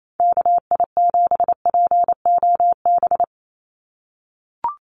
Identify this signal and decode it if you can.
Morse code, heard as a keyed tone, about 28 wpm, 700 Hz. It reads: KI7POB
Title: USA Call Signs 28wpm